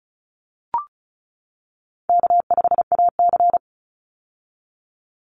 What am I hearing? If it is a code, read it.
Morse code, heard as a keyed tone, about 34 wpm, 700 Hz. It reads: K5AC
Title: USA Call Signs 35wpm